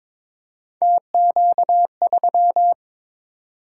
Morse code, heard as a keyed tone, about 22 wpm, 700 Hz. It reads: TQ3
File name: practice / 22wpm / Three Letter-Number Combinations - 22wpm